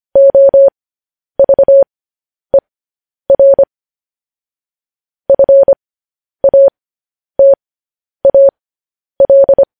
Morse code, heard as a keyed tone, about 25 wpm, spaced as for 10 wpm, 550 Hz. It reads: OVER FATAL
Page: News Headlines in Morse Code at 10 WPM